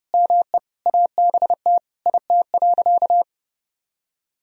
Morse code, heard as a keyed tone, about 32 wpm, 700 Hz. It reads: ME ABT IT.
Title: Sentences from Top 200 Words and Abbreviations 30wpm